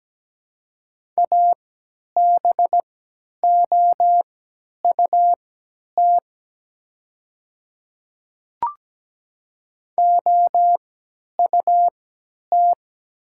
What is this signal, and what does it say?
Morse code, heard as a keyed tone, about 17 wpm, spaced as for 10 wpm, 700 Hz. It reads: ABOUT OUT
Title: 3x Character Spacing - Top 100 Words - 17wpm